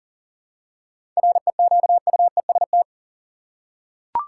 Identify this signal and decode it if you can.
Morse code, heard as a keyed tone, about 40 wpm, 700 Hz. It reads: REQUEST
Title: Seven Letter Words - 40wpm